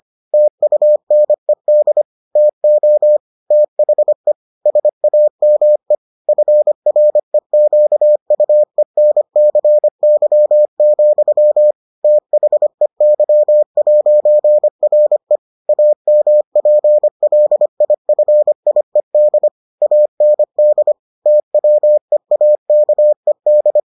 Morse code, heard as a keyed tone, about 25 wpm, 600 Hz. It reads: TUNED TO THE SAME FREQUENCY, THEY'RE AMPLIFIED AND TWEAKED